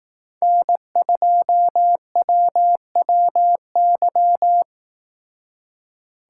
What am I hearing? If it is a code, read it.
Morse code, heard as a keyed tone, about 18 wpm, 700 Hz. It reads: N2WWY
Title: Speed Racing - USA 1x3 Call Signs 15wpm